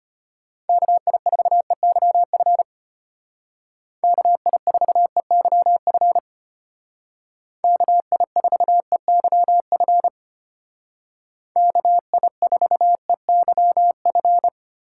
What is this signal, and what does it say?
Morse code, heard as a keyed tone, about 33 wpm, 700 Hz. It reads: KI4EYF KI4EYF KI4EYF KI4EYF